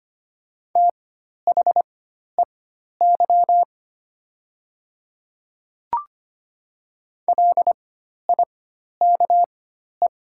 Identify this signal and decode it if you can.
Morse code, heard as a keyed tone, about 25 wpm, spaced as for 12 wpm, 700 Hz. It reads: THEY LIKE
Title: 4x Character Spacing - Top 200 Words - 25wpm